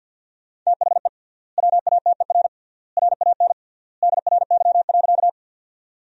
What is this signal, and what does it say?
Morse code, heard as a keyed tone, about 50 wpm, 700 Hz. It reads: THE WATER RAN DRY.
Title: 3x Word-Spacing - Sentences from Top 500 Words - 50wpm